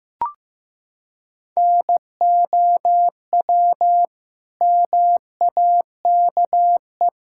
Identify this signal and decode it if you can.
Morse code, heard as a keyed tone, about 16 wpm, 700 Hz. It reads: NOW MAKE
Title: Sentences from Top 1000 Words - Review Part 6 - 15wpm